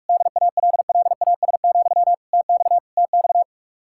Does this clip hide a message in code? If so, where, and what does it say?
Yes: Morse code from 0.1 to 3.4 s, heard as a keyed tone, about 50 wpm, 700 Hz. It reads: DALLAS, TX TX